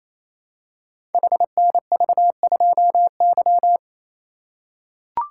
Morse code, heard as a keyed tone, about 27 wpm, 700 Hz. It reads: HNV2Y
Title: Code Groups - 5 Letter:Numbers - 28wpm